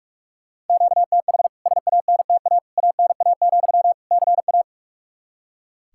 Morse code, heard as a keyed tone, about 43 wpm, 700 Hz. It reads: QTH SANTA ANA, CA
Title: K1USN SST QSO Practice 45wpm